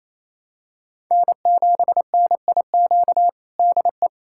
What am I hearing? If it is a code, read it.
Morse code, heard as a keyed tone, about 27 wpm, 700 Hz. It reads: N7NIQ DE
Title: USA Callsign DE Callsign - 5 Characters - 28wpm